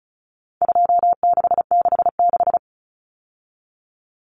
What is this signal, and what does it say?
Morse code, heard as a keyed tone, about 34 wpm, 700 Hz. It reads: 2666
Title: Common Cut Numbers - 4-Digit 35wpm